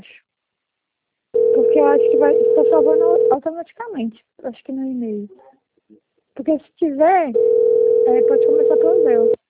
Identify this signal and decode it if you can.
Ringback tone (North America), 440 + 480 Hz, with speech.
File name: out- - - - - .3179